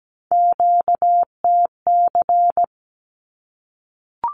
Morse code, heard as a keyed tone, about 17 wpm, 700 Hz. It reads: QTC
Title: Q-Codes 17wpm